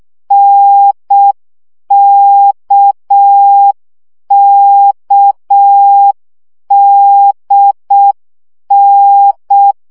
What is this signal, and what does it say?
Morse code, heard as a keyed tone, about 6 wpm, 800 Hz. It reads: NKKDN
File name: CW gyakorlas n d k 5 baud